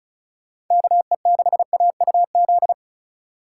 Morse code, heard as a keyed tone, about 44 wpm, 700 Hz. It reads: KE6AUZ
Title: Speed Racing - USA 2x3 Call Signs 35wpm